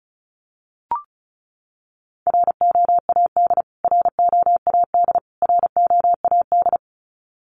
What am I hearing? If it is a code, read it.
Morse code, heard as a keyed tone, about 34 wpm, 700 Hz. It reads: ROAD ROAD ROAD
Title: Top 500 Words Repeated 35wpm